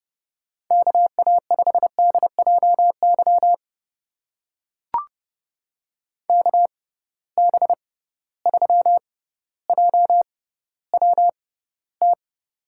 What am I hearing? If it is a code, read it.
Morse code, heard as a keyed tone, about 30 wpm, spaced as for 15 wpm, 700 Hz. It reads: KA5DJY KB3JWT
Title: USA Call Signs - 6 Characters - 6x Character Spacing - 30wpm